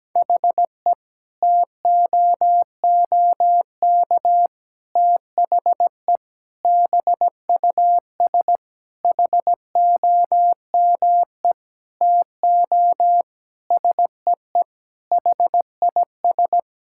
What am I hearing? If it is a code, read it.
Morse code, heard as a keyed tone, about 17 wpm, 700 Hz. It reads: HE TOOK THE BUS HOME TO SEE HIS
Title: Sentences from Top 1000 Words - Review Part 6 - 17wpm